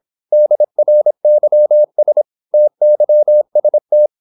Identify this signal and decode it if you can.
Morse code, heard as a keyed tone, about 26 wpm, 600 Hz. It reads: DRYS TYST
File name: Korte ord DK 26wpm